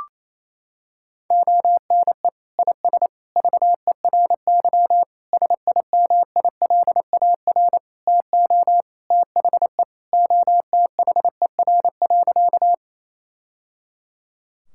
Morse code, heard as a keyed tone, about 27 wpm, 700 Hz. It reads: ONE IS VERY SIMILAR TO THE OTHER.
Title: Sentences from Top 1000 Words - Review Part 5 - 28wpm